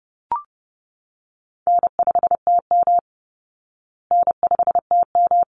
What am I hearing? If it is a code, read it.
Morse code, heard as a keyed tone, about 30 wpm, 700 Hz. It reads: N5TM N5TM
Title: Call Signs Repeated 3x - 30wpm